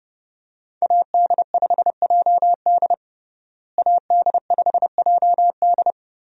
Morse code, heard as a keyed tone, about 32 wpm, 700 Hz. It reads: AD5JD AD5JD
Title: USA Call Signs - 5 Characters - Repeated 2x - 30wpm